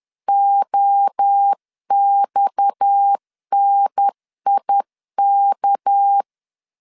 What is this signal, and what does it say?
Morse code, heard as a keyed tone, about 10 wpm, 800 Hz. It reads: OXNIK
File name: CW / DET / 1 6 1500C S C 10wpm 4252